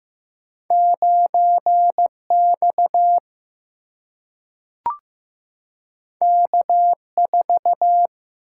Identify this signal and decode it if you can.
Morse code, heard as a keyed tone, about 15 wpm, 700 Hz. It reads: 9X K4
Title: Single-Letter Single-Number 15wpm